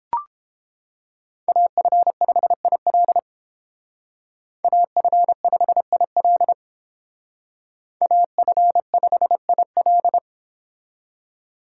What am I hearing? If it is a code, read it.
Morse code, heard as a keyed tone, about 30 wpm, 700 Hz. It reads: AF5IL AF5IL AF5IL